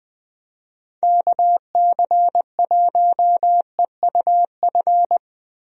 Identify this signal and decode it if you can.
Morse code, heard as a keyed tone, about 21 wpm, 700 Hz. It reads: KC1EUF
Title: USA Call Signs - 6 Characters - Repeated 2x - 20wpm